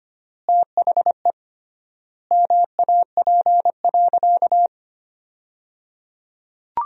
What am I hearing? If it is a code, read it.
Morse code, heard as a keyed tone, about 25 wpm, 700 Hz. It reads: THE MAP.
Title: 3x Word-Spacing - Sentences from Top 400 Words - 25wpm